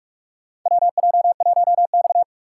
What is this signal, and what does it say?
Morse code, heard as a keyed tone, about 46 wpm, 700 Hz. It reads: WJ1X
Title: USA Call Signs - 4 Characters - 3x Character Spacing - 45wpm